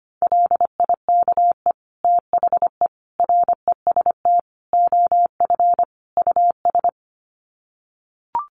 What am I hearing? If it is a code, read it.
Morse code, heard as a keyed tone, about 25 wpm, 700 Hz. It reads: LIKE THE REST OF US
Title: Sets of 5 Words Encore2 25wpm